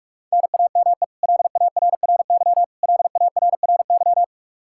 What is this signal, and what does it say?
Morse code, heard as a keyed tone, about 42 wpm, 700 Hz. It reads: NAME LARRY LARRY
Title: Practice QSO - Exchange 1 - Beginning - 45wpm